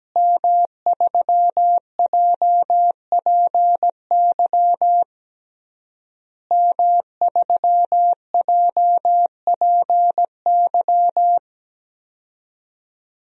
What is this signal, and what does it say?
Morse code, heard as a keyed tone, about 17 wpm, 700 Hz. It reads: M3JPY M3JPY
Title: DXCC-100 Call Signs - 5 Characters - Repeated 3x - 17wpm